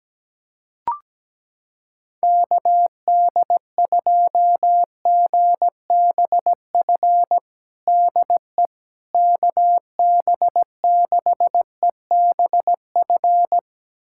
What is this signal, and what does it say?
Morse code, heard as a keyed tone, about 17 wpm, 700 Hz. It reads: KD2GBF DE KB6EBF
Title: USA Callsign DE Callsign 17wpm